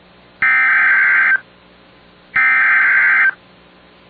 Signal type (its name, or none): Emergency Alert System (SAME) tones